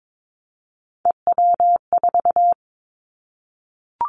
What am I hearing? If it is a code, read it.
Morse code, heard as a keyed tone, about 22 wpm, 700 Hz. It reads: EW4
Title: Callsign Prefixes for Top 100 DXCC - 22wpm